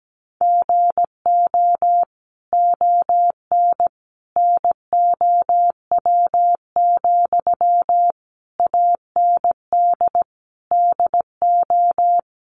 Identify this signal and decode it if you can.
Morse code, heard as a keyed tone, about 17 wpm, 700 Hz. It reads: GO ON NOW, AND DO
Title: Sentences from Top 1500 Words 17wpm